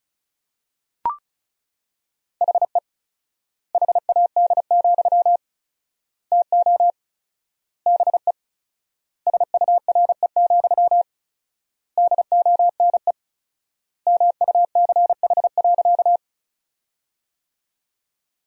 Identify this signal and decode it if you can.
Morse code, heard as a keyed tone, about 34 wpm, 700 Hz. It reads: HE HAD, TO BE SURE, DONE MUCH.
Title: 4x Word-Spacing - Sentences from Top 500 Words - 35wpm